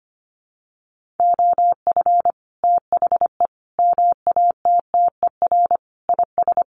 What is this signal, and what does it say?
Morse code, heard as a keyed tone, about 25 wpm, 700 Hz. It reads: OF THE MATTER IS
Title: Sets of 5 Words Encore 25wpm